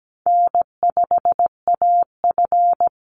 Morse code, heard as a keyed tone, about 17 wpm, 700 Hz. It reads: N5AF
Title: Easy Call Signs Repeated 2x - 17wpm